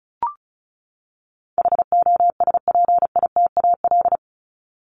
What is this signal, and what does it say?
Morse code, heard as a keyed tone, about 34 wpm, 700 Hz. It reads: HOSPITAL
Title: Top 1751-2000 Words 35wpm